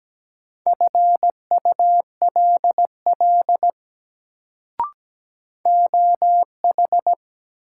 Morse code, heard as a keyed tone, about 17 wpm, 700 Hz. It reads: FULL OH
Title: Sentences from Top 500 Words - New Word Drill - 17wpm